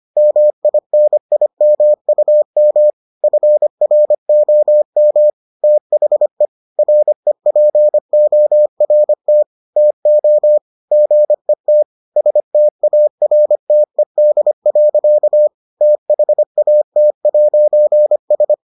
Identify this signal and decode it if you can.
Morse code, heard as a keyed tone, about 25 wpm, 600 Hz. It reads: MINIMUM FROM THE REPORT TO GET STARTED. THAT'S